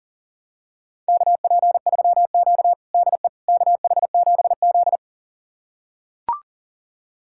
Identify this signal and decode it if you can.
Morse code, heard as a keyed tone, about 40 wpm, 700 Hz. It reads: KP3Q DE KH7Z
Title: USA Callsign DE Callsign - 4 Characters - 40wpm